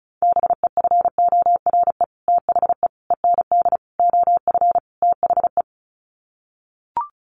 Morse code, heard as a keyed tone, about 34 wpm, 700 Hz. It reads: BEFORE THE END OF THE
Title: Sets of 5 Words Encore 35wpm